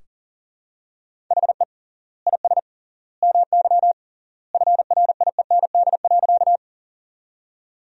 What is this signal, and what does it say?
Morse code, heard as a keyed tone, about 37 wpm, 700 Hz. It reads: HE IS MY FRIEND.
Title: 3x Word-Spacing - Sentences from Top 300 Words - 40wpm